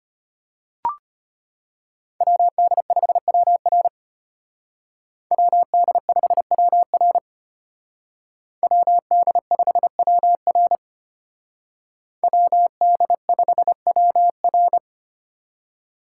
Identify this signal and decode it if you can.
Morse code, heard as a keyed tone, about 32 wpm, 700 Hz. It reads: WD5WR WD5WR WD5WR WD5WR